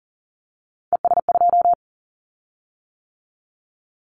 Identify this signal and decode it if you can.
Morse code, heard as a keyed tone, about 38 wpm, 700 Hz. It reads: ES2
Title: Callsign Prefixes for Top 100 DXCC - 40wpm